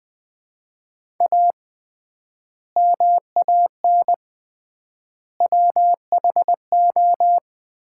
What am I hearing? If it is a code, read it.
Morse code, heard as a keyed tone, about 20 wpm, 700 Hz. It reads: A MAN WHO
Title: 3x Word-Spacing - Sets of 3 Words - 20wpm